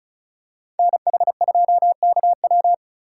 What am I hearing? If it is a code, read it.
Morse code, heard as a keyed tone, about 36 wpm, 700 Hz. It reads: NH2KW
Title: USA Call Signs - 5 Characters - 3x Character Spacing - 35wpm